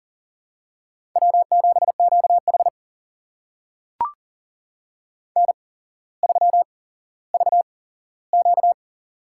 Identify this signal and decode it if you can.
Morse code, heard as a keyed tone, about 40 wpm, spaced as for 17 wpm, 700 Hz. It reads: W7QH N3VQ
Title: USA Call Signs - 4 Characters - 8x Character Spacing - 40wpm